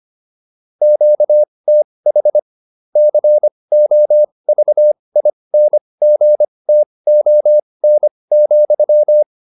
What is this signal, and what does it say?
Morse code, heard as a keyed tone, about 25 wpm, spaced as for 20 wpm, 600 Hz. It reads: QTH COVINGTON,